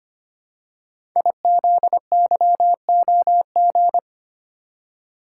Morse code, heard as a keyed tone, about 25 wpm, 700 Hz. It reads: IZYOG
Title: Code Groups - 5 Letters - 25wpm